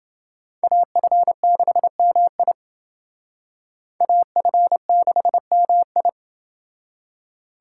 Morse code, heard as a keyed tone, about 29 wpm, 700 Hz. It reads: AF6MI AF6MI